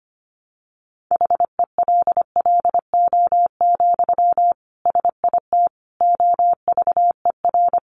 Morse code, heard as a keyed tone, about 25 wpm, 700 Hz. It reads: HELLO, SIT OVER
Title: Sentences from Top 1100 Words 25wpm